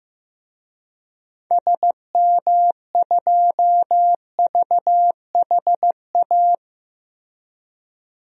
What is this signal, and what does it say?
Morse code, heard as a keyed tone, about 15 wpm, 700 Hz. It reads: SM2VHA
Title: DXCC-100 Call Signs - 6 Characters - Repeated 2x - 15wpm